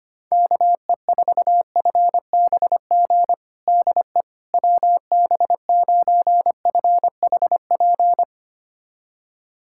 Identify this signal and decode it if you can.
Morse code, heard as a keyed tone, about 25 wpm, 700 Hz. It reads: KE4FBG DE WB9FHP
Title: USA Callsign DE Callsign - 6 Characters - 25wpm